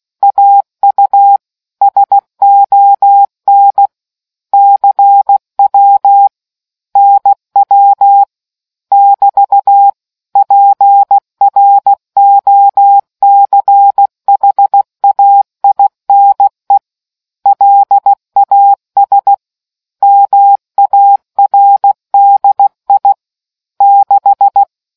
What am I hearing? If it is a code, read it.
Morse code, heard as a keyed tone, about 16 wpm, 800 Hz. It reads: AU SON CW NW <BT> PROCHAINE LAS MARDI 6